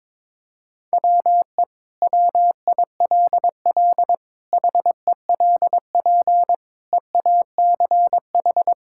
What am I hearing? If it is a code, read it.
Morse code, heard as a keyed tone, about 22 wpm, 700 Hz. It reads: WE WILL HELP EACH
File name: practice / 22wpm / Sentences with 2-Word Phrases - 22wpm